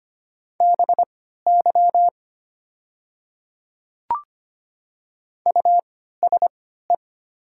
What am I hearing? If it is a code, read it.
Morse code, heard as a keyed tone, about 25 wpm, spaced as for 14 wpm, 700 Hz. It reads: BY USE
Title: 3x Character Spacing - Top 100 Words - 25wpm